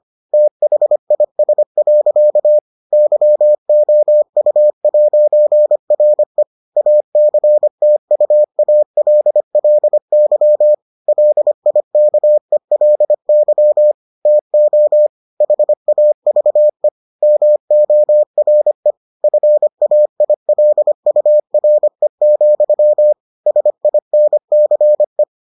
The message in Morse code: THIS. YOU'RE ACTUALLY LIKELY TO HAVE MORE FAILURE, SINCE

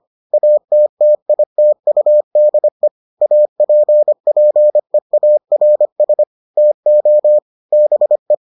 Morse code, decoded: ATTITUDE APPEARS TO BE

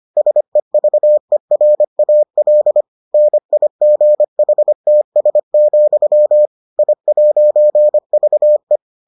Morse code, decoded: SEVERAL NIGHTS, I'VE

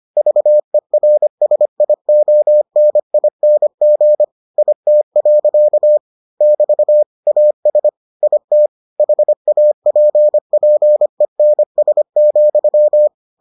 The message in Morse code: VERSIONING IT. <BT> AS IT HAPPENS,

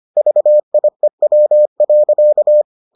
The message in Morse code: VIEW.

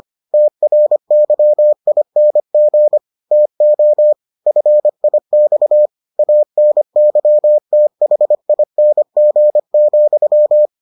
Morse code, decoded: TRYING TO FIX ANYTHING,

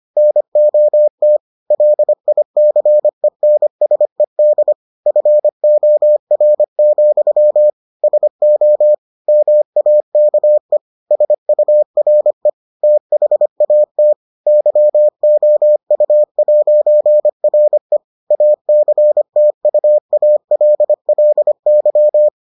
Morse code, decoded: NOT LICENSED FOR, SO MAKE SURE THAT YOU'RE ACTUALLY

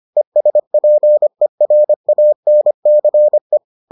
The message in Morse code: ESPERANCE

